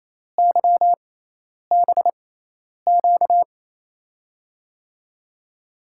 Morse code, heard as a keyed tone, about 28 wpm, spaced as for 10 wpm, 700 Hz. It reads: YBQ
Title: USA Call Signs - 5 Characters - 6x Character Spacing - 28wpm